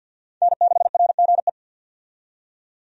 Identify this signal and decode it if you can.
Morse code, heard as a keyed tone, about 51 wpm, 700 Hz. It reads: N6RGE
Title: USA Call Signs - 5 Characters - 50wpm